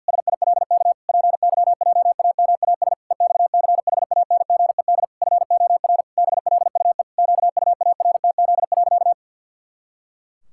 Morse code, heard as a keyed tone, about 53 wpm, 700 Hz. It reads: SILK PYJAMAS EXCHANGED FOR BLUE QUARTZ.
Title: Pangram Challenge v2 - 50wpm